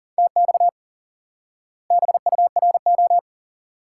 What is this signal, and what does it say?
Morse code, heard as a keyed tone, about 40 wpm, 700 Hz. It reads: TX BURO
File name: IQE Course 3 - Lesson 89 - YAGI - Two Elements - 40wpm